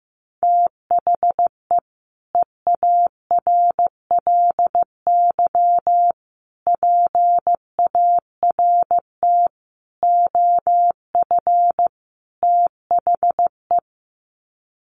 Morse code, heard as a keyed tone, about 15 wpm, 700 Hz. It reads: THE EARLY PART OF THE
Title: Sets of 5 Words Encore2 15wpm